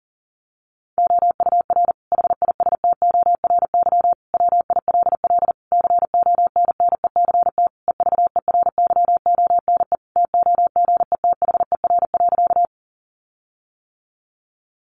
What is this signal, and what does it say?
Morse code, heard as a keyed tone, about 38 wpm, 700 Hz. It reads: OUR HISTORY WILL CONNECT EVERYONE TOGETHER.
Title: Sentences from Top 1500 Words 40wpm